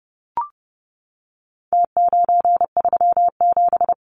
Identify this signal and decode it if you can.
Morse code, heard as a keyed tone, about 30 wpm, 700 Hz. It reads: T937
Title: Numbers - Contest Style - 4-Digit 30wpm